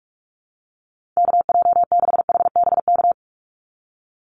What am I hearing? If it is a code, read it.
Morse code, heard as a keyed tone, about 44 wpm, 700 Hz. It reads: KJ6HBK
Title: USA Call Signs 45wpm